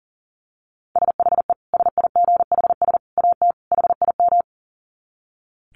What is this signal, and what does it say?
Morse code, heard as a keyed tone, about 37 wpm, 700 Hz. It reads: SHE SIGHS AT HIM
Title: Dit Challenge 40wpm